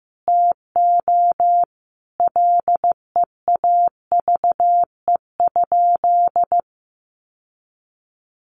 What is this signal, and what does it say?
Morse code, heard as a keyed tone, about 15 wpm, 700 Hz. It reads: TO LEAVE?
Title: Sentences from Top 1100 Words 15wpm